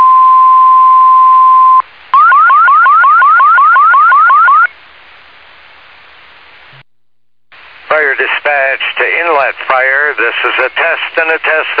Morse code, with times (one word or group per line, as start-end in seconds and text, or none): none